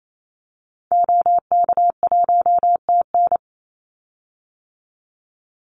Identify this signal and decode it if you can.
Morse code, heard as a keyed tone, about 28 wpm, 700 Hz. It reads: OK1TN
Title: Call Signs Repeated 2x - 28wpm